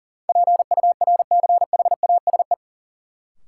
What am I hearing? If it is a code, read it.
Morse code, heard as a keyed tone, about 40 wpm, 700 Hz. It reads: PURCHASE
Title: Sentences from Top 800 Words - New Word Drill - 40wpm